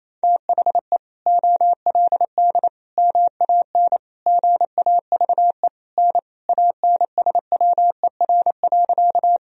Morse code, heard as a keyed tone, about 28 wpm, 700 Hz. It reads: THE OLD MAN GAVE N ANSWER.
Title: Sentences from Top 500 Words and Abbreviations 28wpm